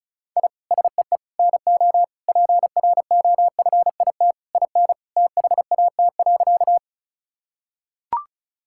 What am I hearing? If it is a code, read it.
Morse code, heard as a keyed tone, about 34 wpm, 700 Hz. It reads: I SEE NO PROFIT IN THAT.
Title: Sentences from Top 1000 Words - Review Part 5 - 35wpm